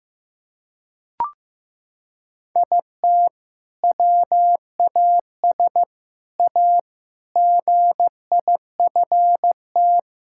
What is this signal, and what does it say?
Morse code, heard as a keyed tone, about 15 wpm, 700 Hz. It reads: IT WAS A GIFT
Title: Sentences from Top 1000 Words - Review Part 6 - 15wpm